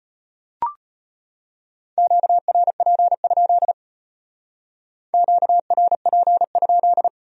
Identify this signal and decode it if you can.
Morse code, heard as a keyed tone, about 33 wpm, 700 Hz. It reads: QRP? QRP?